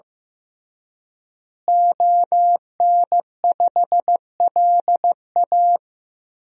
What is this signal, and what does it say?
Morse code, heard as a keyed tone, about 16 wpm, 700 Hz. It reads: ON5LA
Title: DXCC-100 Call Signs - 5 Characters - Repeated 3x - 15wpm